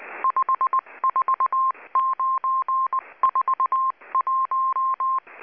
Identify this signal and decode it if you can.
Morse code, heard as a keyed tone, about 19 wpm, 1060 Hz. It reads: 54941
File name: HM02- -0540utc